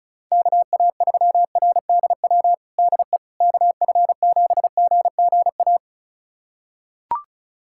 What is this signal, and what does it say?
Morse code, heard as a keyed tone, about 34 wpm, 700 Hz. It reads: KA3RDW DE KF7GGA
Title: USA Callsign DE Callsign 35wpm